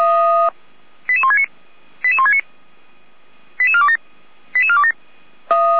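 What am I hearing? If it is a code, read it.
Five-tone selective call (ZVEI), heard as quick runs of five tones, tones of about 70 ms. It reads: 88169 88169 88427 88427
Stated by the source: Five-tone ZVEI selective call 88352 received by SDR, 455 channel.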